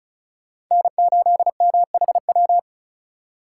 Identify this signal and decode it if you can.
Morse code, heard as a keyed tone, about 46 wpm, spaced as for 41 wpm, 700 Hz. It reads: N8MHW
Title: Speed Racing - USA 1x3 Call Signs 35wpm